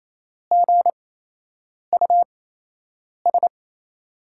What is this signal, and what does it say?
Morse code, heard as a keyed tone, about 28 wpm, spaced as for 8 wpm, 700 Hz. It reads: GUS
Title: USA Call Signs - 5 Characters - 8x Character Spacing - 28wpm